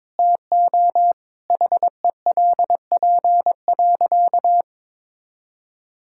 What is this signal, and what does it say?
Morse code, heard as a keyed tone, about 21 wpm, 700 Hz. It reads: TO HELP.
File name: Sentences with 7-Word Phrases - 22wpm